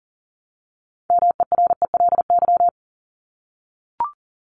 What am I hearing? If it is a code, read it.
Morse code, heard as a keyed tone, about 40 wpm, 700 Hz. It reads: MERELY
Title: Top 1251-1500 Words 40wpm